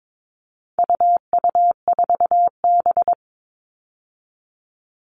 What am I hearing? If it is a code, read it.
Morse code, heard as a keyed tone, about 22 wpm, 700 Hz. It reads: UU4B
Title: All Cut Numbers - 4-Digit 22wpm